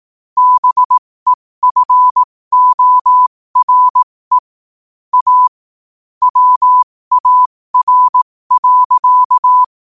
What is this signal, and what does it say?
Morse code, heard as a keyed tone, about 18 wpm, spaced as for 15 wpm, 1000 Hz. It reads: BEFORE A WAR.
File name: qotdfiles / 15wpm